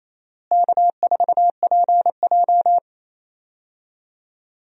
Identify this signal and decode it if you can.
Morse code, heard as a keyed tone, about 28 wpm, 700 Hz. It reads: K4PJ
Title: USA Call Signs - 4 Characters - Repeated 3x - 28wpm